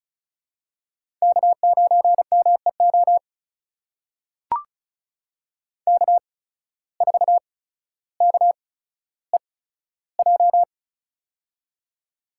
Morse code, heard as a keyed tone, about 35 wpm, spaced as for 15 wpm, 690 Hz. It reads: K9MEO K4KEJ
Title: USA Call Signs - 5 Characters - 8x Character Spacing - 35wpm